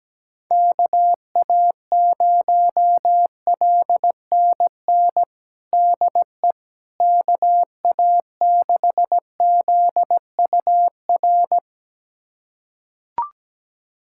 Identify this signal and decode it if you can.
Morse code, heard as a keyed tone, about 17 wpm, 700 Hz. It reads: KA0LNN DE KA6ZUR
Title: USA Callsign DE Callsign - 6 Characters - 17wpm